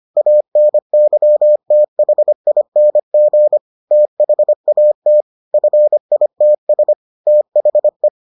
Morse code, heard as a keyed tone, about 25 wpm, 600 Hz. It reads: ANYTHING THAT FITS THE